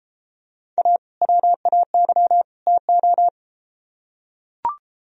Morse code, heard as a keyed tone, about 33 wpm, 700 Hz. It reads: A WAY TO